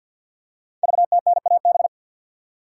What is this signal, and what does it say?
Morse code, heard as a keyed tone, about 52 wpm, 700 Hz. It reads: VTNAB
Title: Code Groups - 5 Letters - 50wpm